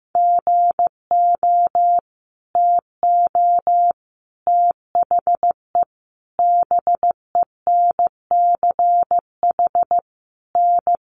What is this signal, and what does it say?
Morse code, heard as a keyed tone, about 15 wpm, 700 Hz. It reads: GO TO THE BENCH N
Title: Sentences from Top 1400 Words 15wpm